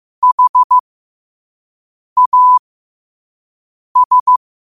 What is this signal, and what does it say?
Morse code, heard as a keyed tone, about 15 wpm, spaced as for 5 wpm, 1000 Hz. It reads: HAS
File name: 5wpm